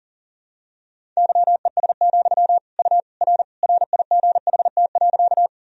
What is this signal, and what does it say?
Morse code, heard as a keyed tone, about 40 wpm, 700 Hz. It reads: YES, U R RIGHT.
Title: Sentences from Top 500 Words and Abbreviations 40wpm